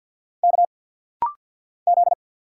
Morse code, heard as a keyed tone, about 47 wpm, 690 Hz. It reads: K Z
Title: Single Letter-Number Alt - Rapid-Fire 50wpm